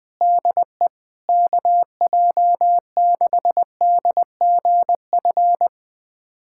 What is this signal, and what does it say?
Morse code, heard as a keyed tone, about 21 wpm, 700 Hz. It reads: DE KJ6DGF
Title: USA Callsign DE Callsign - 6 Characters - 20wpm